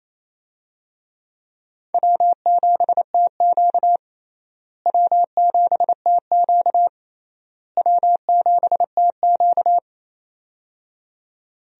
Morse code, heard as a keyed tone, about 28 wpm, 700 Hz. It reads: W7TQ W7TQ W7TQ
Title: USA Call Signs - 4 Characters - Repeated 3x - 28wpm